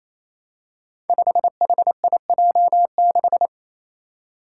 Morse code, heard as a keyed tone, about 27 wpm, 700 Hz. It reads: 5HIJ6
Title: Five Challenging Letter-Number Combinations 28wpm